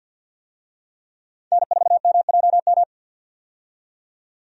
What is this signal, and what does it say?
Morse code, heard as a keyed tone, about 49 wpm, 700 Hz. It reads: N4MJM
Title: USA Call Signs - 5 Characters - 50wpm